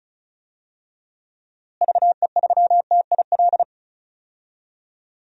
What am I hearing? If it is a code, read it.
Morse code, heard as a keyed tone, about 36 wpm, 700 Hz. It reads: VE3TIL
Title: DXCC-100 Call Signs - 6 Characters - Repeated 3x - 35wpm